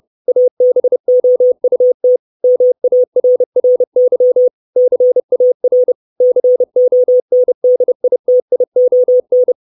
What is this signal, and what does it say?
Morse code, heard as a keyed tone, about 30 wpm, 500 Hz. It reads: ABOUT MARRY CAR CONDITION
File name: 632 Words 30WPM